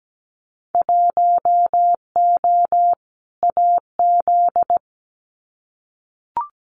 Morse code, heard as a keyed tone, about 17 wpm, 690 Hz. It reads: 1O AZ
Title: Winter Field Day Exchanges 17wpm